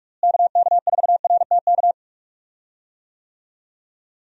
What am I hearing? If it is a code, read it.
Morse code, heard as a keyed tone, about 46 wpm, 700 Hz. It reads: KK4RTK
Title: USA Call Signs - 6 Characters - Repeated 2x - 45wpm